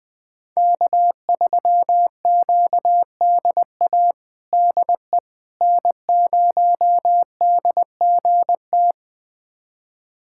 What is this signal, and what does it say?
Morse code, heard as a keyed tone, about 20 wpm, 700 Hz. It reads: K3QDA DE N0DGT
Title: USA Callsign DE Callsign - 5 Characters - 20wpm